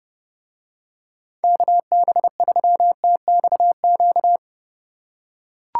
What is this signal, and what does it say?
Morse code, heard as a keyed tone, about 30 wpm, 700 Hz. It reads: KB3TXQ